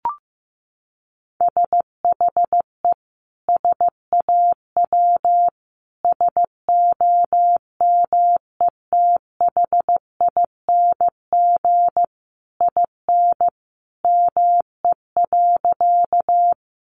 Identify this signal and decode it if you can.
Morse code, heard as a keyed tone, about 15 wpm, 700 Hz. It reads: SHE SAW SOMETHING IN ME.
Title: Sentences from Top 600 Words 15wpm